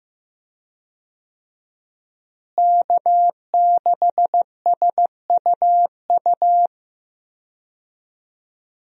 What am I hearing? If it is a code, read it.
Morse code, heard as a keyed tone, about 15 wpm, 700 Hz. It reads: K6SUU
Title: USA Call Signs - 5 Characters - Repeated 3x - 15wpm